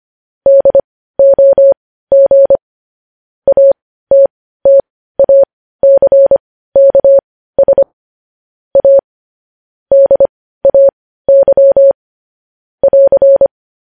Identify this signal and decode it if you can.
Morse code, heard as a keyed tone, about 25 wpm, spaced as for 15 wpm, 550 Hz. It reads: DOG ATTACKS A DAY <AR>
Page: news-headlines-in-morse-code-at-15-wpm